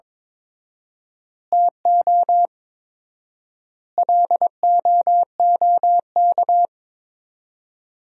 Morse code, heard as a keyed tone, about 22 wpm, 700 Hz. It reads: TO LOOK
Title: 4x Word-Spacing - Sentences from Top 100 Words - 22wpm